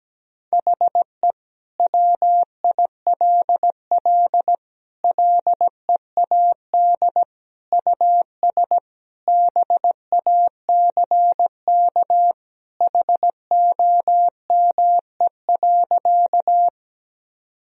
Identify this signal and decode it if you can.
Morse code, heard as a keyed tone, about 17 wpm, 700 Hz. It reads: HE WILL LEAD US BACK HOME.
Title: Sentences from Top 1000 Words - Review Part 2 - 17wpm